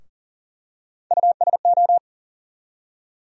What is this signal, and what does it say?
Morse code, heard as a keyed tone, about 40 wpm, 700 Hz. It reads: USO
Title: USA Call Sign Postfixes - 3 Letters 40wpm